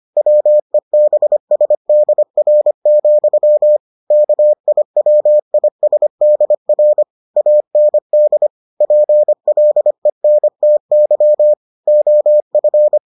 Morse code, heard as a keyed tone, about 25 wpm, 600 Hz. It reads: WEBSDR, KIWISDR AND PLENTY OF